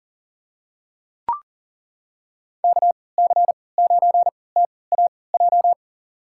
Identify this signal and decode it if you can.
Morse code, heard as a keyed tone, about 39 wpm, spaced as for 22 wpm, 700 Hz. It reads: KC9TAJ
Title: USA Call Signs - 3x Character Spacing - 40wpm